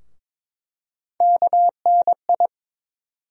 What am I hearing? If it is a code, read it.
Morse code, heard as a keyed tone, about 22 wpm, 700 Hz. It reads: KNI
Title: USA Call Sign Postfixes - 3 Letters 22wpm